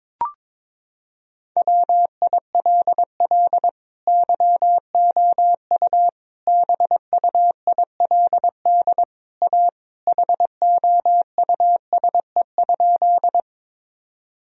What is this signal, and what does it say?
Morse code, heard as a keyed tone, about 23 wpm, 700 Hz. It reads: WILL YOU BUILD A HOUSE?
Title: Sentences from Top 1000 Words - Review Part 1 - 22wpm